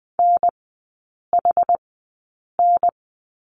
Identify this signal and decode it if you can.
Morse code, heard as a keyed tone, about 20 wpm, spaced as for 8 wpm, 700 Hz. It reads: NHN
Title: 093 Lesson 16e - Letter c - Three Characters 20wpm